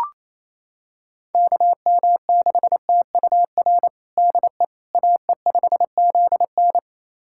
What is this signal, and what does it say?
Morse code, heard as a keyed tone, about 27 wpm, 700 Hz. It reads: KM6TUR DE AE5ZN
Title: USA Callsign DE Callsign 28wpm